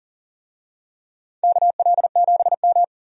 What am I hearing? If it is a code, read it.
Morse code, heard as a keyed tone, about 40 wpm, 700 Hz. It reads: KL7M
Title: DXCC-100 Call Signs - 4 Characters - 40wpm